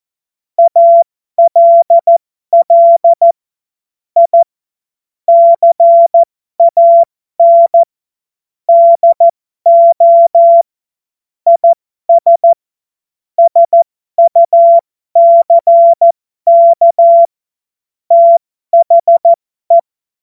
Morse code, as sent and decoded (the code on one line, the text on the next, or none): .- .-.. .-.. / .. / -.-. .- -. / -.. --- / .. ... / ... ..- -.-. -.- / - .... .
ALL I CAN DO IS SUCK THE